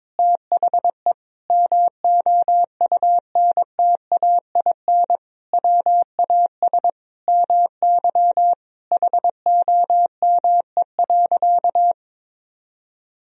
Morse code, heard as a keyed tone, about 22 wpm, 700 Hz. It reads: THE MOUNTAIN WAS MY HOME.
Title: Sentences from Top 1000 Words - Review Part 1 - 22wpm